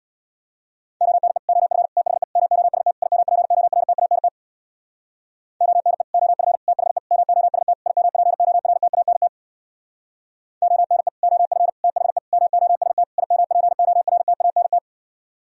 Morse code, heard as a keyed tone, about 64 wpm, 700 Hz. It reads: ONE OF THE MOST IMPORTANT ONE OF THE MOST IMPORTANT ONE OF THE MOST IMPORTANT